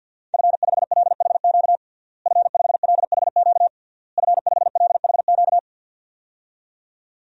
Morse code, heard as a keyed tone, about 50 wpm, 700 Hz. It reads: F5LHQ F5LHQ F5LHQ
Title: DXCC-100 Call Signs - 5 Characters - Repeated 3x - 50wpm